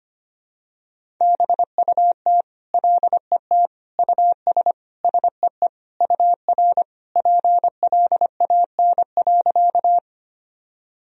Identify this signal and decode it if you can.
Morse code, heard as a keyed tone, about 25 wpm, 700 Hz. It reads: BUT LET US SEE UR PLAN.
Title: Sentences from Top 500 Words and Abbreviations 25wpm